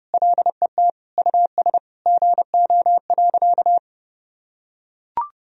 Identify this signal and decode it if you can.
Morse code, heard as a keyed tone, about 30 wpm, 700 Hz. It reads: LET US GO.
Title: Sentences from Top 1000 Words - Review Part 2 - 30wpm